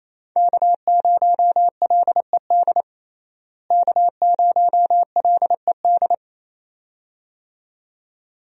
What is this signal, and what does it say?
Morse code, heard as a keyed tone, about 28 wpm, 700 Hz. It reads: K0LED K0LED
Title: USA Call Signs - 5 Characters - Repeated 2x - 28wpm